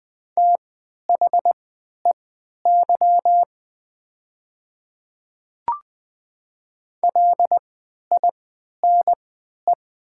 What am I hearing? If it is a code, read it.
Morse code, heard as a keyed tone, about 20 wpm, spaced as for 11 wpm, 700 Hz. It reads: THEY LINE
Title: 3x Character Spacing - Top 200 Words - 20wpm